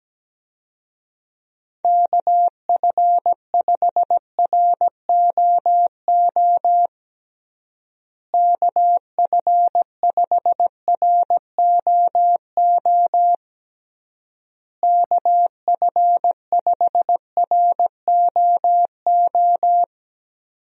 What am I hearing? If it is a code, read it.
Morse code, heard as a keyed tone, about 17 wpm, 700 Hz. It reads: KF5ROO KF5ROO KF5ROO
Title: USA Call Signs - 6 Characters - Repeated 3x - 17wpm